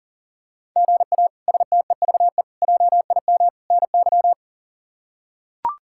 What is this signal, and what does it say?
Morse code, heard as a keyed tone, about 40 wpm, 700 Hz. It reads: GA STEVE JIM NY
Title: K1USN SST QSO Practice - 40wpm